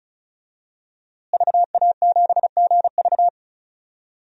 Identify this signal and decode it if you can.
Morse code, heard as a keyed tone, about 34 wpm, 700 Hz. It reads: VA7GV
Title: DXCC-100 Call Signs - 5 Characters - Repeated 2x - 35wpm